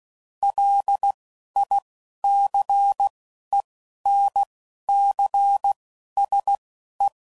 Morse code, heard as a keyed tone, about 16 wpm, spaced as for 11 wpm, 800 Hz. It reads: LICENCSE
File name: CW Code Groups @10WPM / QSOs / QSO kc7sac